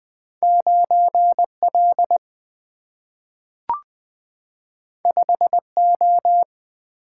Morse code, heard as a keyed tone, about 21 wpm, 700 Hz. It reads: 9L 5O
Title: Single-Letter Single-Number 20wpm